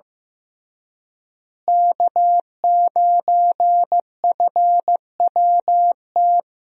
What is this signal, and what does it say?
Morse code, heard as a keyed tone, about 14 wpm, 700 Hz. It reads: K9FWT
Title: USA Call Signs - 5 Characters - Repeated 3x - 15wpm